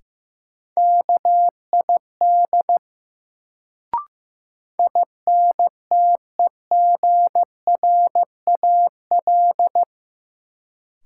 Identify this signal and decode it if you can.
Morse code, heard as a keyed tone, about 15 wpm, 700 Hz. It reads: KID INTEGRAL
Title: Sets of 5 Words - New Word Drill - 15wpm